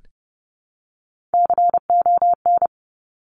Morse code, heard as a keyed tone, about 30 wpm, 700 Hz. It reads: CON
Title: Top Three Letter Combinations in Words 30wpm